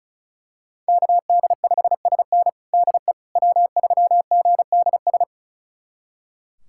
Morse code, heard as a keyed tone, about 36 wpm, 700 Hz. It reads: KD5SN DE W3GDS
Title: USA Callsign DE Callsign 35wpm